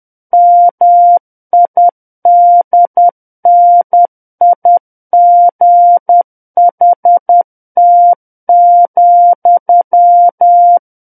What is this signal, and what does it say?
Morse code, heard as a keyed tone, about 10 wpm, 700 Hz. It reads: MIDNIGHT,
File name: Book One / 01-Chapter 1 - 10 WPM